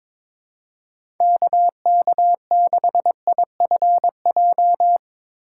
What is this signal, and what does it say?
Morse code, heard as a keyed tone, about 21 wpm, 700 Hz. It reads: KK6IFJ
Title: USA Call Signs - 2x Character Spacing - 22wpm